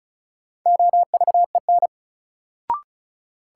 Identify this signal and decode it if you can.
Morse code, heard as a keyed tone, about 36 wpm, 700 Hz. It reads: OVEN
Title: Sentences from Top 1200 Words - New Word Drill Alt - 35wpm